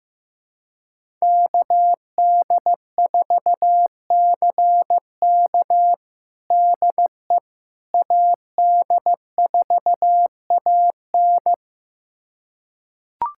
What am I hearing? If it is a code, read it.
Morse code, heard as a keyed tone, about 15 wpm, 700 Hz. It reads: KD4CK DE AD4AN
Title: USA Callsign DE Callsign - 5 Characters - 15wpm